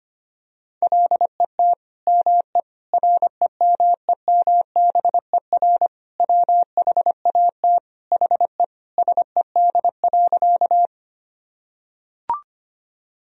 Morse code, heard as a keyed tone, about 25 wpm, 700 Hz. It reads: LET ME REMEMBER WHAT HE SED.
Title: Sentences from Top 400 Words and Abbreviations 25wpm